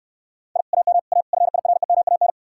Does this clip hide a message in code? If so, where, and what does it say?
Yes: Morse code from 0.6 to 2.3 s, heard as a keyed tone, about 69 wpm, 700 Hz. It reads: I AM A FIREMAN